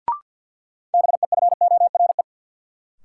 Morse code, heard as a keyed tone, about 54 wpm, 700 Hz. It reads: BEFORE
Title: Sets of 2 Words - New Word Drill - 50wpm